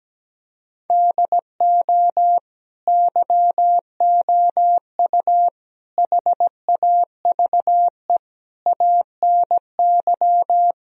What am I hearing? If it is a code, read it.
Morse code, heard as a keyed tone, about 17 wpm, 700 Hz. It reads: DO YOU HAVE ANY
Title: Speed-Racing - Sets of 4 Words 17wpm